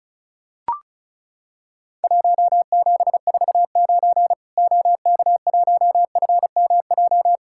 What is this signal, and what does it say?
Morse code, heard as a keyed tone, about 36 wpm, 700 Hz. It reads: 1749 OK1FMJ
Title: Worked All Europe DX Contest - QTC Exchange - 35wpm